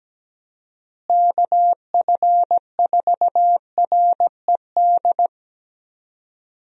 Morse code, heard as a keyed tone, about 17 wpm, 700 Hz. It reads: KF4RED
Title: USA Call Signs - 6 Characters - Repeated 3x - 17wpm